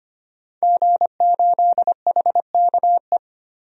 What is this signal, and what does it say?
Morse code, heard as a keyed tone, about 25 wpm, 700 Hz. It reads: G8HKE
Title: DXCC-100 Call Signs - 5 Characters - Repeated 3x - 25wpm